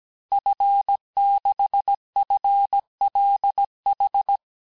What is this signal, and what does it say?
Morse code, heard as a keyed tone, about 17 wpm, 800 Hz. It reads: F6FLH
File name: corrige-cours-1581-fin-du-QSO-vitesse-16-mpm